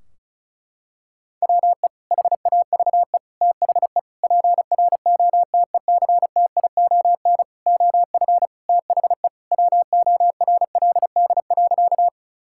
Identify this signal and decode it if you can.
Morse code, heard as a keyed tone, about 34 wpm, 700 Hz. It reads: WE HAVE THE PROTECTION OF THE WORLD.
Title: Sentences from Top 1600 Words 35wpm